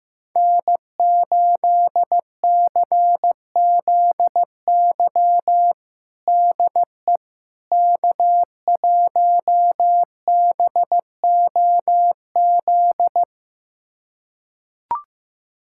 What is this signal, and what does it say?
Morse code, heard as a keyed tone, about 15 wpm, 700 Hz. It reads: N8CZY DE K1BOZ
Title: USA Callsign DE Callsign - 5 Characters - 15wpm